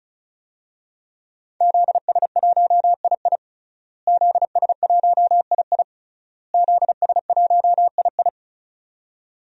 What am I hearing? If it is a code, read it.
Morse code, heard as a keyed tone, about 34 wpm, 700 Hz. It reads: ZS1II ZS1II ZS1II
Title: DXCC-100 Call Signs - 5 Characters - Repeated 3x - 35wpm